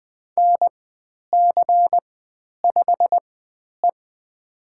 Morse code, heard as a keyed tone, about 20 wpm, spaced as for 10 wpm, 700 Hz. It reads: NC5E
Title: 109 Lesson 18g - Letter m - Call Signs 20wpm